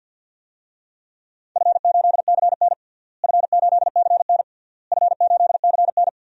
Morse code, heard as a keyed tone, about 51 wpm, 700 Hz. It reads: F8CN F8CN F8CN
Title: DXCC-100 Call Signs - 4 Characters - Repeated 3x - 50wpm